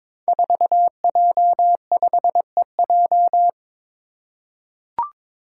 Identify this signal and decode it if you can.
Morse code, heard as a keyed tone, about 22 wpm, 700 Hz. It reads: 4J5EJ
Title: Five Challenging Letter-Number Combinations 22wpm